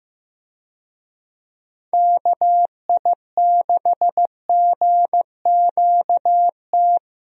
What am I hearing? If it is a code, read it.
Morse code, heard as a keyed tone, about 15 wpm, 700 Hz. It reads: KI6GQT
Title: USA Call Signs - 6 Characters - Repeated 2x - 15wpm